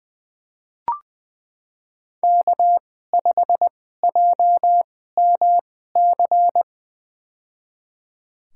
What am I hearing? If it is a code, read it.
Morse code, heard as a keyed tone, about 20 wpm, spaced as for 14 wpm, 700 Hz. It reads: K5JMC
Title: USA Call Signs - 5 Characters - 2x Character Spacing - 20wpm